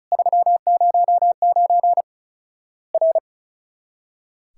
Morse code, heard as a keyed tone, about 35 wpm, 700 Hz. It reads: 309 R
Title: Worked All Europe DX Contest - QTC Exchange - 35wpm